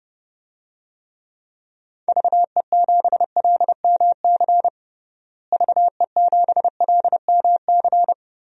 Morse code, heard as a keyed tone, about 30 wpm, 700 Hz. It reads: VE7LMC VE7LMC
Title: DXCC-100 Call Signs - 6 Characters - Repeated 3x - 30wpm